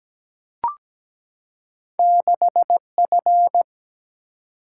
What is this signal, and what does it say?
Morse code, heard as a keyed tone, about 17 wpm, 700 Hz. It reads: 6F
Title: Single-Letter Single-Number 17wpm